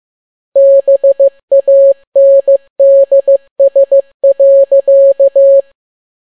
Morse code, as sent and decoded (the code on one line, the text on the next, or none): -... .- -. -.. ... .-.-.-
BANDS.